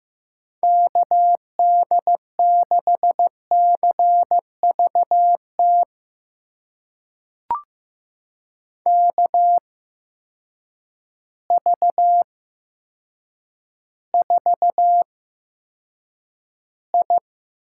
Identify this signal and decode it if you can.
Morse code, heard as a keyed tone, about 15 wpm, spaced as for 8 wpm, 700 Hz. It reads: KD6CVT KV4I
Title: USA Call Signs - 8x Character Spacing - 15wpm